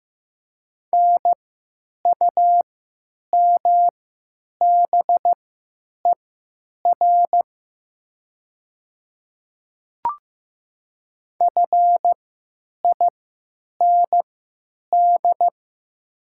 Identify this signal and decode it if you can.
Morse code, heard as a keyed tone, about 15 wpm, spaced as for 9 wpm, 700 Hz. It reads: NUMBER FIND
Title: 3x Character Spacing - Top 100 Words - 15wpm